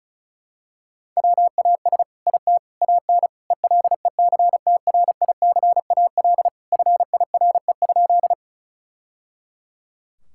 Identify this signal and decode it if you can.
Morse code, heard as a keyed tone, about 36 wpm, 700 Hz. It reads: WAS IT AN ELECTRICAL FIRE?
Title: Sentences from Top 1000 Words - Review Part 6 - 35wpm